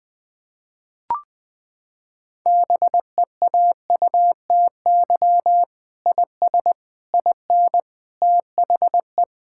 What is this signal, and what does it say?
Morse code, heard as a keyed tone, about 21 wpm, 700 Hz. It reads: BEAUTY IS IN THE
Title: Sentences with 6-Word Phrases - 20wpm